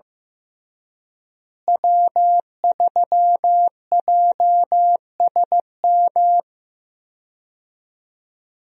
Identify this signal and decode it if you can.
Morse code, heard as a keyed tone, about 15 wpm, 700 Hz. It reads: W3JSM
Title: USA Call Signs - 5 Characters - Repeated 3x - 15wpm